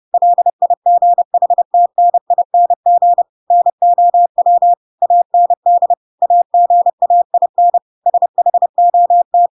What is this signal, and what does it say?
Morse code, heard as a keyed tone, about 30 wpm, 700 Hz. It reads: LIGHTNING NOW AND AGAIN SHOT
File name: Pinocchio / Pinocchio ch6 30 30 ik1hge